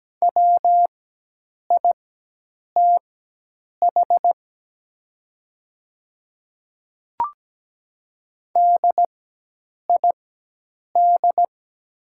Morse code, heard as a keyed tone, about 17 wpm, spaced as for 8 wpm, 700 Hz. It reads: WITH DID
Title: 4x Character Spacing - Top 100 Words - 17wpm